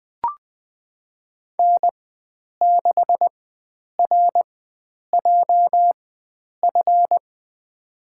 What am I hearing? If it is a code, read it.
Morse code, heard as a keyed tone, about 20 wpm, spaced as for 9 wpm, 700 Hz. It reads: N6RJF
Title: USA Call Signs - 5 Characters - 4x Character Spacing - 20wpm